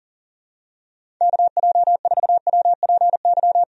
Morse code, heard as a keyed tone, about 40 wpm, 700 Hz. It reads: KJ4WPY
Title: USA Call Signs - 6 Characters - 40wpm